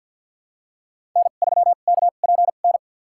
Morse code, heard as a keyed tone, about 53 wpm, spaced as for 37 wpm, 700 Hz. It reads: N3KPN
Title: USA Call Signs - 5 Characters - 2x Character Spacing - 50wpm